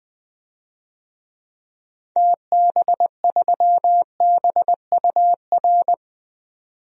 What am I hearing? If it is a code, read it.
Morse code, heard as a keyed tone, about 20 wpm, 700 Hz. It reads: TB3BUR
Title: DXCC-100 Call Signs - 6 Characters - 20wpm